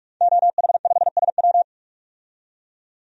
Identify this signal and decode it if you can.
Morse code, heard as a keyed tone, about 44 wpm, 700 Hz. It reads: OH5SW
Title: DXCC-100 Call Signs - 5 Characters - 45wpm